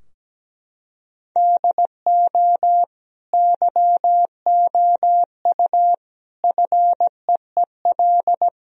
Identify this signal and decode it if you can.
Morse code, heard as a keyed tone, about 17 wpm, 700 Hz. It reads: DO YOU FEEL
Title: Sentences from Top 1000 Words - Review Part 3 - 17wpm